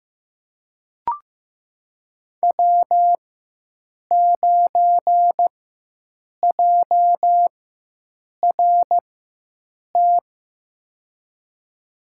Morse code, heard as a keyed tone, about 15 wpm, spaced as for 7 wpm, 700 Hz. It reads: W9JRT
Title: USA Call Signs - 5 Characters - 4x Character Spacing - 15wpm